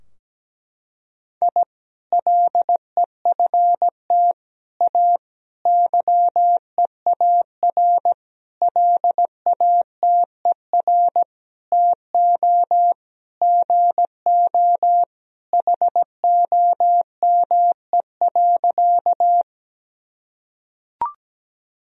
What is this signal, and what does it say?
Morse code, heard as a keyed tone, about 17 wpm, 700 Hz. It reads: I LEFT A YEAR LATER TO GO HOME.
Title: Sentences from Top 1000 Words - Review Part 4 - 17wpm